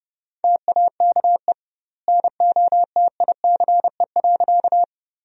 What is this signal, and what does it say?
Morse code, heard as a keyed tone, about 30 wpm, 700 Hz. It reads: TAKE NOTICE.
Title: 2x Word-Spacing - Sentences from Top 400 Words - 30wpm